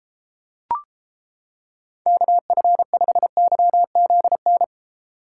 Morse code, heard as a keyed tone, about 29 wpm, 700 Hz. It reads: KF5YZN